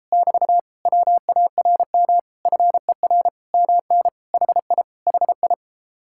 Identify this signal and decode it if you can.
Morse code, heard as a keyed tone, about 33 wpm, 700 Hz. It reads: <BT> WARM FER MN HI HI